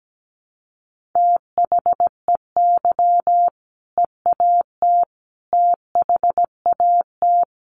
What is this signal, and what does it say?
Morse code, heard as a keyed tone, about 17 wpm, 700 Hz. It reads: THEY EAT THAT
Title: Sentences from Top 1400 Words 17wpm